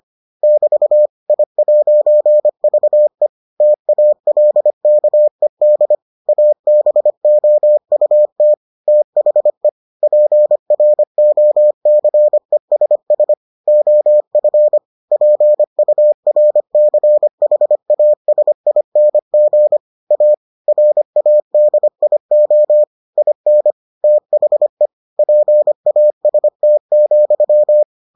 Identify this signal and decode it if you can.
Morse code, heard as a keyed tone, about 25 wpm, 600 Hz. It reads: <BT> I'VE TALKED ABOUT THE PROCESS OF PURCHASING A RADIO IN THE PAST,